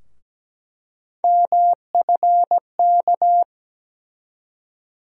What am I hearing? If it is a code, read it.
Morse code, heard as a keyed tone, about 17 wpm, 700 Hz. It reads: MFK
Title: USA Call Sign Postfixes - 3 Letters 17wpm